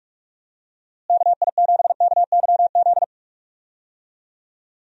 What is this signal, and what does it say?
Morse code, heard as a keyed tone, about 46 wpm, 700 Hz. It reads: KI7KYZ
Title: USA Call Signs - 6 Characters - Repeated 2x - 45wpm